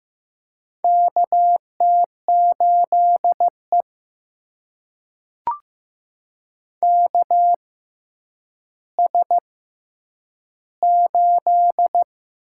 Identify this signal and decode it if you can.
Morse code, heard as a keyed tone, about 15 wpm, spaced as for 9 wpm, 700 Hz. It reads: KT8E KS8
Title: USA Call Signs - 4 Characters - 6x Character Spacing - 15wpm